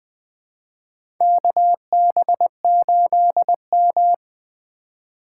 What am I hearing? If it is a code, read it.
Morse code, heard as a keyed tone, about 20 wpm, 700 Hz. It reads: KB8M
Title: K1USN SST QSO Practice - 20wpm